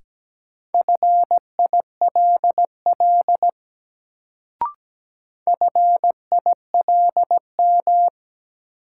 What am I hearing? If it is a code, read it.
Morse code, heard as a keyed tone, about 17 wpm, 700 Hz. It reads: FILL FILM
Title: Sentences from Top 1000 Words - Review Part 3 - New Word Drill - 17wpm